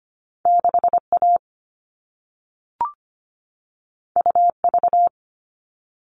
Morse code, heard as a keyed tone, about 25 wpm, 700 Hz. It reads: 6A UV
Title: All Cut Numbers - 2-Digit 25wpm